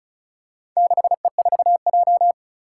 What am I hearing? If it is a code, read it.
Morse code, heard as a keyed tone, about 36 wpm, 700 Hz. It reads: 6E4J
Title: Four Challenging Letter-Number Combinations 35wpm